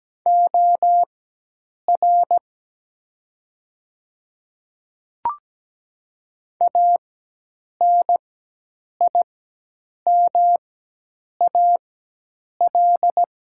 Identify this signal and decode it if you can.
Morse code, heard as a keyed tone, about 17 wpm, spaced as for 8 wpm, 700 Hz. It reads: OR ANIMAL
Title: 4x Character Spacing - Top 200 Words - 17wpm